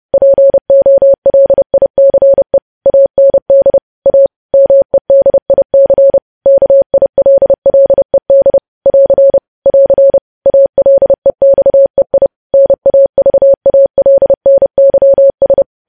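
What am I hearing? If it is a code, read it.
Morse code, heard as a keyed tone, about 30 wpm, 550 Hz. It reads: POLICE AND A MEDIC KILLED <AR> <AR> ALEXEI NAVALNYS
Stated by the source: Morse code news headlines practice sent at 30 wpm.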